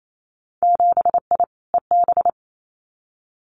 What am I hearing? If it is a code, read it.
Morse code, heard as a keyed tone, about 28 wpm, 700 Hz. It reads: 7I EB
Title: Winter Field Day Exchanges Encore 28wpm